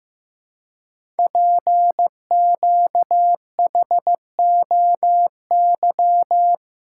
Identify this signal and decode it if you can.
Morse code, heard as a keyed tone, about 15 wpm, 700 Hz. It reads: PQHOY
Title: Code Groups - 5 Letters - 15wpm